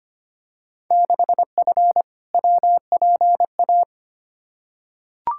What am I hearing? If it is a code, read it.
Morse code, heard as a keyed tone, about 25 wpm, 700 Hz. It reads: 6F WPA
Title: Field Day Exchanges Encore - 25wpm